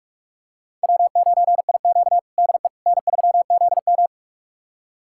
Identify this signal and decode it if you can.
Morse code, heard as a keyed tone, about 46 wpm, 700 Hz. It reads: W9IQ DE N3ZM
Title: USA Callsign DE Callsign - 4 Characters - 45wpm